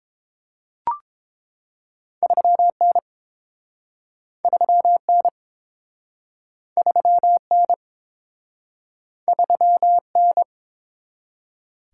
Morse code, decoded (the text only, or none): 3N 3N 3N 3N